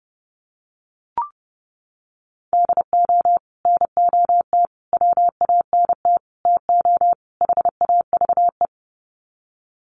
Morse code, heard as a keyed tone, about 30 wpm, 700 Hz. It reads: DO NOT WANT TO HAVE
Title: Sets of 5 Words 30wpm